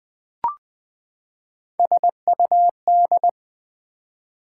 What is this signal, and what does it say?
Morse code, heard as a keyed tone, about 20 wpm, 700 Hz. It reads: SUD
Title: Build A Word - 3 Letters At A Time - 6-Letter Words - 20wpm